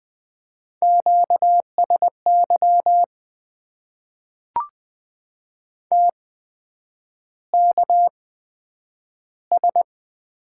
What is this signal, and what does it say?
Morse code, heard as a keyed tone, about 20 wpm, spaced as for 9 wpm, 700 Hz. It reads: QSY TKS
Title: Top 90 QSO Elements - 8x Character Spacing - 20wpm